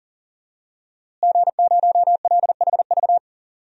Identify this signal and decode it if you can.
Morse code, heard as a keyed tone, about 38 wpm, 700 Hz. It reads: G0LHV
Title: DXCC-100 Call Signs - 5 Characters - Repeated 3x - 40wpm